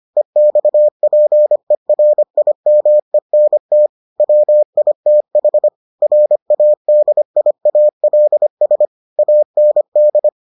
Morse code, heard as a keyed tone, about 25 wpm, 600 Hz. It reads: EXPERIMENT WITH RADIALS AND